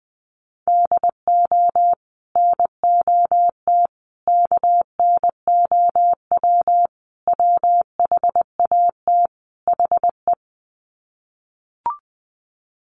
Morse code, heard as a keyed tone, about 20 wpm, 700 Hz. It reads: DO NOT KNOW WHAT HE
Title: Sets of 5 Words Encore2 20wpm